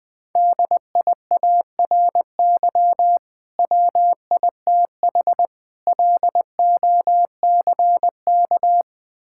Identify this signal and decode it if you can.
Morse code, heard as a keyed tone, about 20 wpm, 700 Hz. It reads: DIARY WITH LOCK
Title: Difficult Words - Speed-Racing with Hints - 20wpm